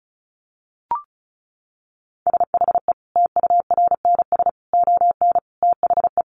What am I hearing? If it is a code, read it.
Morse code, heard as a keyed tone, about 34 wpm, 700 Hz. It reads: SHE TURNS ON THE
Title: Easy Phrases 35wpm